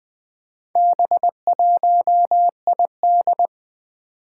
Morse code, heard as a keyed tone, about 21 wpm, 700 Hz. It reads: B1ID
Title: Four Challenging Letter-Number Combinations 20wpm